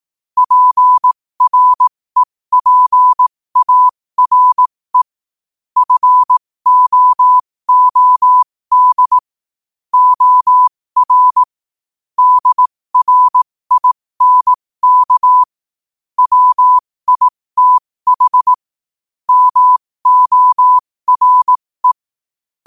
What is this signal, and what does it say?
Morse code, heard as a keyed tone, about 18 wpm, spaced as for 15 wpm, 1000 Hz. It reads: PREPARE FOOD OR DRINK WITH MORE
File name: qotdfiles / 15wpm